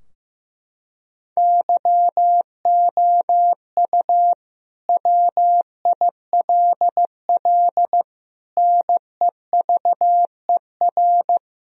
Morse code, heard as a keyed tone, about 15 wpm, 700 Hz. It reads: YOU WILL NEVER
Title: Sentences from Top 1000 Words - Review Part 2 - 15wpm